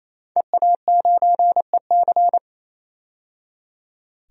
Morse code, heard as a keyed tone, about 28 wpm, 700 Hz. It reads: EA9EC
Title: DXCC-100 Call Signs - 5 Characters - 28wpm